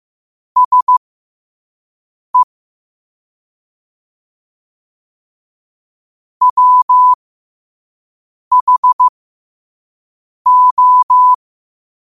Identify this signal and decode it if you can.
Morse code, heard as a keyed tone, about 15 wpm, spaced as for 5 wpm, 1000 Hz. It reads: SE WHO
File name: qotdfiles / 5wpm